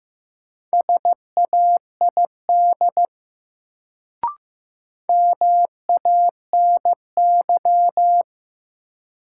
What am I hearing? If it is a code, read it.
Morse code, heard as a keyed tone, about 16 wpm, 700 Hz. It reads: SAID MANY
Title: Sentences from Top 100 Words - New Word Drill - 15wpm